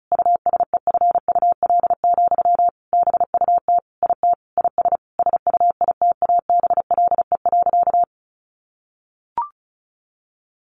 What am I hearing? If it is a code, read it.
Morse code, heard as a keyed tone, about 34 wpm, 700 Hz. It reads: USEFUL, BUT IT IS SUITABLE.
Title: Sentences from Top 1400 Words 35wpm